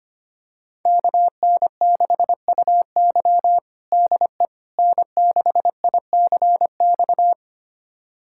Morse code, heard as a keyed tone, about 25 wpm, 700 Hz. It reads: KN6UY DE N6ICX
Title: USA Callsign DE Callsign - 5 Characters - 25wpm